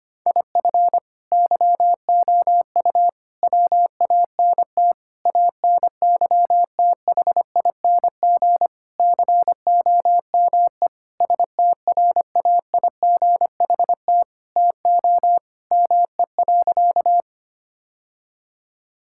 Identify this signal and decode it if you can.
Morse code, heard as a keyed tone, about 25 wpm, 700 Hz. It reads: IF YOU WANT ANYTHING COME STRAIGHT TO ME.
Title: Sentences from Top 1000 Words - Review Part 6 - 25wpm